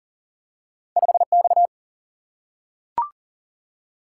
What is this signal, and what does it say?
Morse code, heard as a keyed tone, about 40 wpm, 700 Hz. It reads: 5X
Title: Single-Letter Single-Number 40wpm